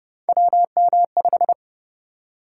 Morse code, heard as a keyed tone, about 27 wpm, 700 Hz. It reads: WM5